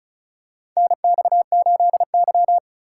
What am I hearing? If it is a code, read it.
Morse code, heard as a keyed tone, about 36 wpm, 700 Hz. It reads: NX8Y
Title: USA Call Signs - 4 Characters - 3x Character Spacing - 35wpm